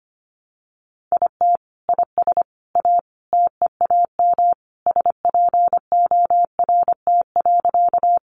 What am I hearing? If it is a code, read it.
Morse code, heard as a keyed tone, about 25 wpm, 700 Hz. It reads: IT IS A TEAM SPORT.
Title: Sentences from Top 1500 Words 25wpm